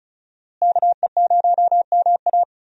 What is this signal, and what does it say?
Morse code, heard as a keyed tone, about 36 wpm, 700 Hz. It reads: KE0MA
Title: USA Call Signs - 5 Characters - Repeated 3x - 35wpm